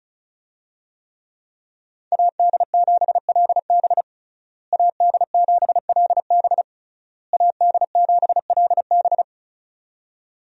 Morse code, heard as a keyed tone, about 36 wpm, 700 Hz. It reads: AD7LB AD7LB AD7LB
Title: USA Call Signs - 5 Characters - Repeated 3x - 35wpm